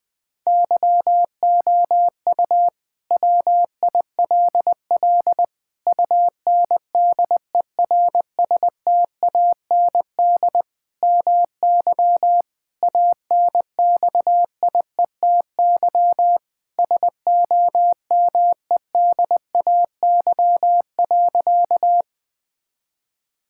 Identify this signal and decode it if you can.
Morse code, heard as a keyed tone, about 20 wpm, 700 Hz. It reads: YOU WILL UNDERSTAND MY ANXIETY SOMEDAY.
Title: Sentences from Top 1600 Words 20wpm